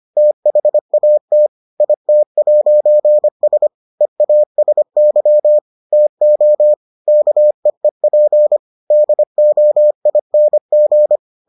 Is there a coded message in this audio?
Yes: Morse code from 0.2 to 11.2 s, heard as a keyed tone, about 25 wpm, 600 Hz. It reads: THAT IT'S EASY TO KEEP DOING